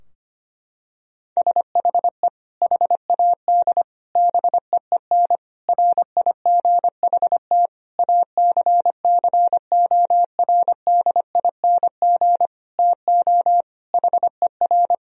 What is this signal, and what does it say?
Morse code, heard as a keyed tone, about 25 wpm, 700 Hz. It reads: SHE HAD BEEN RIGHT ACCORDING TO HER
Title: Sentences from Top 1000 Words - Review Part 4 - 25wpm